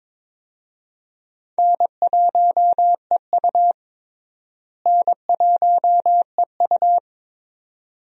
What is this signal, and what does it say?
Morse code, heard as a keyed tone, about 22 wpm, 700 Hz. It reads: N1EU N1EU
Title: USA Call Signs - 4 Characters - Repeated 2x - 22wpm